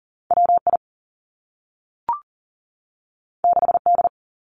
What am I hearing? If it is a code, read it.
Morse code, heard as a keyed tone, about 40 wpm, 700 Hz. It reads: W I 6 D
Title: Two Challenging Letter-Number Combinations 40wpm